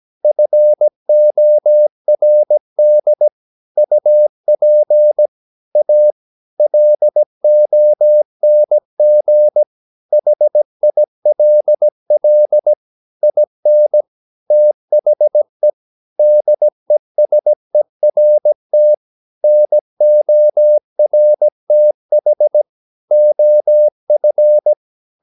Morse code, heard as a keyed tone, about 17 wpm, 600 Hz. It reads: FORD UP A LONG HILL IN THE DESERT NORTH OF
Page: Romanen Never af Ken Follet på engelsk.